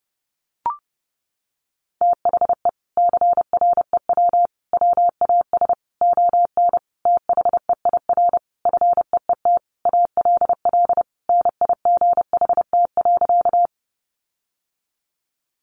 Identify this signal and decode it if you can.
Morse code, heard as a keyed tone, about 30 wpm, 700 Hz. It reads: THE CREW WAS ON THEIR FEET ALL NIGHT.
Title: Sentences from Top 1500 Words 30wpm